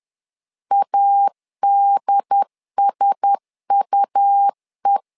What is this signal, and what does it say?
Morse code, heard as a keyed tone, about 10 wpm, 800 Hz. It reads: ADSUE
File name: LEZIONI / 5 LBVC / 5 L 10wpm 2259